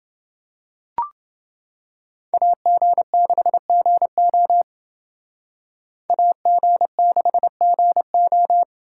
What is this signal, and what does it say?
Morse code, heard as a keyed tone, about 28 wpm, 700 Hz. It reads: AG6GO AG6GO